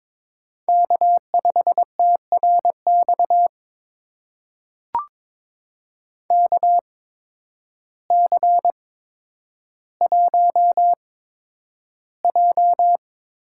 Morse code, heard as a keyed tone, about 22 wpm, spaced as for 10 wpm, 700 Hz. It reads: K5TRX KC1J
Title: USA Call Signs - 8x Character Spacing - 22wpm